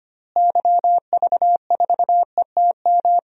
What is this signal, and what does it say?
Morse code, heard as a keyed tone, about 25 wpm, 700 Hz. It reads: YV4ETM
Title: DXCC-100 Call Signs - 6 Characters - Repeated 3x - 25wpm